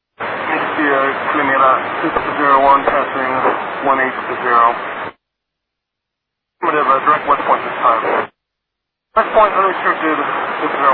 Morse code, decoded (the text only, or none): none